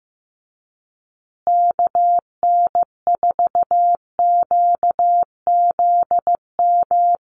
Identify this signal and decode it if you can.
Morse code, heard as a keyed tone, about 15 wpm, 700 Hz. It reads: KN4QZM
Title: USA Call Signs Repeated 3x 15wpm